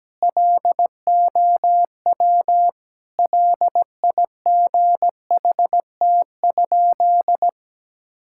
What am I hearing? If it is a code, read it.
Morse code, heard as a keyed tone, about 17 wpm, 700 Hz. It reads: LOW LIGHT?
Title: Sentences from Top 1000 Words - Review Part 5 - 17wpm